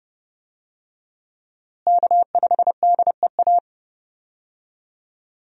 Morse code, heard as a keyed tone, about 32 wpm, 700 Hz. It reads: K5DEA
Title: USA Call Signs - 5 Characters - Repeated 2x - 30wpm